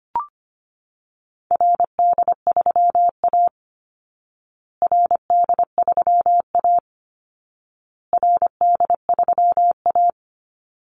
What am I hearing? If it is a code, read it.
Morse code, heard as a keyed tone, about 25 wpm, 700 Hz. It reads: RD3A RD3A RD3A
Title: Call Signs Repeated 2x - 25wpm